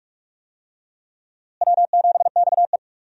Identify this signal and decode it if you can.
Morse code, heard as a keyed tone, about 41 wpm, 700 Hz. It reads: W7XE
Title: USA Call Signs - 4 Characters - Repeated 3x - 45wpm